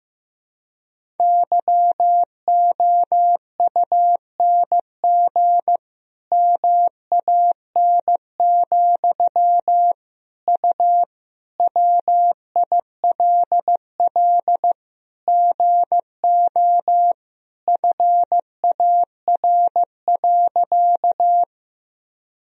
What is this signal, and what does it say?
Morse code, heard as a keyed tone, about 15 wpm, 700 Hz. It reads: YOUNG MAN, U WILL GO FAR.
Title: Sentences from Top 400 Words and Abbreviations 15wpm